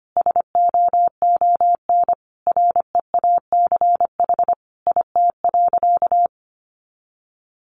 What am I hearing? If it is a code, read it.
Morse code, heard as a keyed tone, about 25 wpm, 700 Hz. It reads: SOON REACH IT.
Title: Sentences from Top 400 Words 25wpm